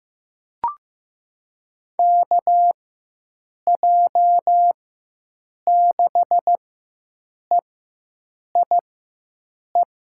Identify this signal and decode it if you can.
Morse code, heard as a keyed tone, about 15 wpm, spaced as for 7 wpm, 700 Hz. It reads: KJ6EIE
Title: USA Call Signs - 6 Characters - 4x Character Spacing - 15wpm